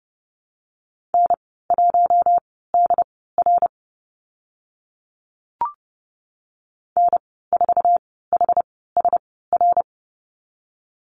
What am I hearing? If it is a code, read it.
Morse code, heard as a keyed tone, about 30 wpm, spaced as for 17 wpm, 700 Hz. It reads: N1DR N4HSR
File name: Lesson 14g - Number 2 - Call Signs 30wpm